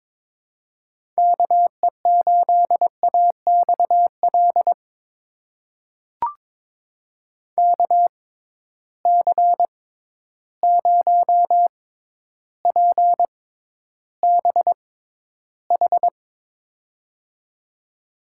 Morse code, heard as a keyed tone, about 22 wpm, spaced as for 11 wpm, 700 Hz. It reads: KE8AXL KC0PBH
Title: USA Call Signs - 6 Characters - 6x Character Spacing - 22wpm